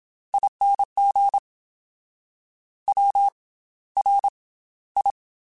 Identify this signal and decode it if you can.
Morse code, heard as a keyed tone, about 27 wpm, spaced as for 15 wpm, 800 Hz. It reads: ING WRI
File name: Special INGs / Special INGs-mp3s / Slow ing @25